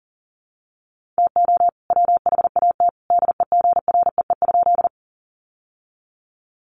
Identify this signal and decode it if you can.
Morse code, heard as a keyed tone, about 40 wpm, 700 Hz. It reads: TO WHAT DEGREE?
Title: Sentences from Top 1500 Words 40wpm